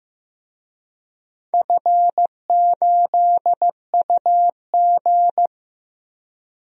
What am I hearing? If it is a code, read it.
Morse code, heard as a keyed tone, about 15 wpm, 700 Hz. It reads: F8UG
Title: DXCC-100 Call Signs - 4 Characters - Repeated 2x - 15wpm